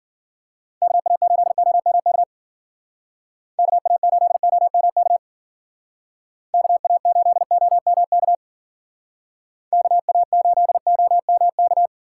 Morse code, heard as a keyed tone, about 52 wpm, 700 Hz. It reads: KA8OMK KA8OMK KA8OMK KA8OMK